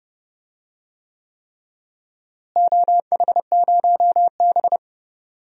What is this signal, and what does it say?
Morse code, heard as a keyed tone, about 29 wpm, 700 Hz. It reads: OH0B
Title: DXCC-100 Call Signs - 4 Characters - Repeated 3x - 30wpm